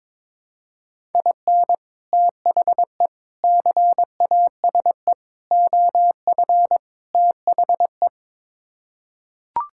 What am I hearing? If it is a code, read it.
Morse code, heard as a keyed tone, about 22 wpm, 700 Hz. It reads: IN THE CASE OF THE
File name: practice / 22wpm / Build A Phrase - 5-Word Phrases - 22wpm